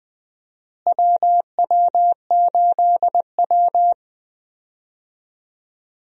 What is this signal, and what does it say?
Morse code, heard as a keyed tone, about 20 wpm, 700 Hz. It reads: WW8W
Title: USA Call Signs - 4 Characters - Repeated 3x - 20wpm